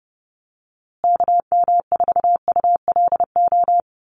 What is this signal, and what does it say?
Morse code, heard as a keyed tone, about 30 wpm, 700 Hz. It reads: KM4ULO
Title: USA Call Signs Repeated 3x 30wpm